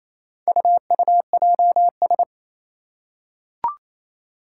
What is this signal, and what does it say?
Morse code, heard as a keyed tone, about 27 wpm, 700 Hz. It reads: UUJS
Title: Four Challenging Letter-Number Combinations 28wpm